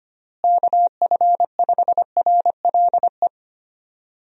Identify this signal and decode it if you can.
Morse code, heard as a keyed tone, about 25 wpm, 700 Hz. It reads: KF5RLE
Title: USA Call Signs - 6 Characters - Repeated 3x - 25wpm